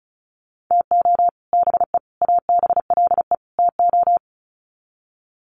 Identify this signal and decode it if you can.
Morse code, heard as a keyed tone, about 34 wpm, 700 Hz. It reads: TO BE ABLE TO
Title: Sets of 4 Words 35wpm